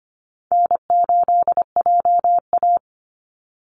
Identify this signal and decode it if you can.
Morse code, heard as a keyed tone, about 25 wpm, 700 Hz. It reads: N8JA
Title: Easy Call Signs - 25wpm